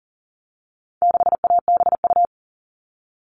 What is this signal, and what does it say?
Morse code, heard as a keyed tone, about 40 wpm, 700 Hz. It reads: 6ABU
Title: All Cut Numbers - 4-Digit 40wpm